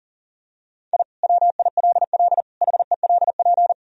Morse code, heard as a keyed tone, about 40 wpm, 700 Hz. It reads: I WILL HELP
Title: Sentences from Top 1000 Words - Review Part 3 - 40wpm